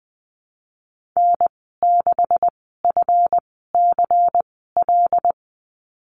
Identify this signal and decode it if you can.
Morse code, heard as a keyed tone, about 20 wpm, spaced as for 15 wpm, 700 Hz. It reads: N6FCL
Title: 154 Lesson 25g - Letter p - Call Signs 20wpm